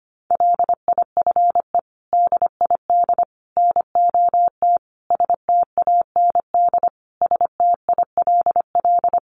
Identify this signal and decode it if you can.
Morse code, heard as a keyed tone, about 25 wpm, 700 Hz. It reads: LIFE DID NOT STAND STILL
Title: Sentences from Top 800 Words 25wpm